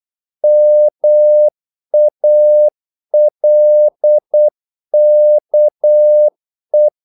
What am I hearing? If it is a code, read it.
Morse code, heard as a keyed tone, about 8 wpm, 600 Hz. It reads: MALKE
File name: Korte ord DK 8wpm